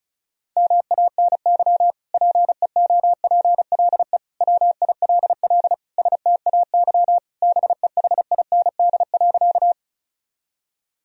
Morse code, heard as a keyed tone, about 36 wpm, 700 Hz. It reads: MANY PEOPLE WILL STAY BEHIND.
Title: Sentences from Top 1000 Words - Review Part 1 - 35wpm